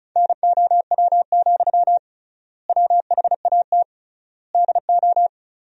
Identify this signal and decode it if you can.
Morse code, heard as a keyed tone, about 34 wpm, 700 Hz. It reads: NOW, WHAT DO
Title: 3x Word-Spacing - Sentences from Top 200 Words - 35wpm